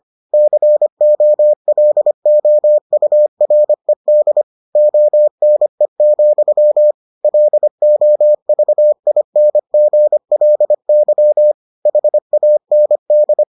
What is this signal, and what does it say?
Morse code, heard as a keyed tone, about 25 wpm, 600 Hz. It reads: COLOURED ONE, LOVINGLY HAND